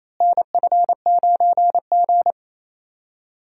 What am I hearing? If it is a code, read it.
Morse code, heard as a keyed tone, about 28 wpm, 700 Hz. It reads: NF9G
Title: USA Call Signs - 4 Characters - Repeated 3x - 28wpm